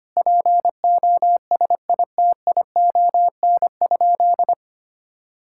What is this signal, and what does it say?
Morse code, heard as a keyed tone, about 25 wpm, 700 Hz. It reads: POSITION?
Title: Sentences from Top 1000 Words - Review Part 5 - 25wpm